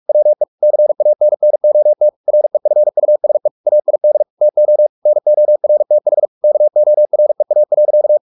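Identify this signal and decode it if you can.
Morse code, heard as a keyed tone, about 44 wpm, 600 Hz. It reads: WE CANNOT REFUSE AID TO NORTH KOREA.
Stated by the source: Morse practice at 45 wpm, the novel Never by Ken Follett in English, from CW med Gnister.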